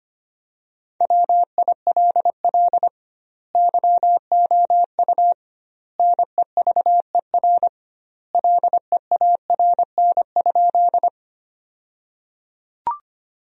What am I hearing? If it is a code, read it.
Morse code, heard as a keyed tone, about 25 wpm, 700 Hz. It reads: WILL YOU NEVER LEARN?
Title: 2x Word-Spacing - Sentences from Top 300 Words - 25wpm